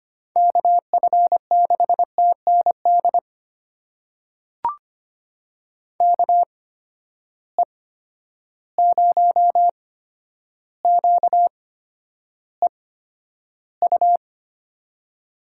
Morse code, heard as a keyed tone, about 25 wpm, spaced as for 11 wpm, 700 Hz. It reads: KF6TND KE0QEU
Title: USA Call Signs - 6 Characters - 8x Character Spacing - 25wpm